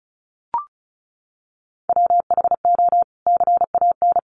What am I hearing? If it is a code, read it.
Morse code, heard as a keyed tone, about 34 wpm, 700 Hz. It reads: WHO CAN
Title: Sentences from Top 300 Words 35wpm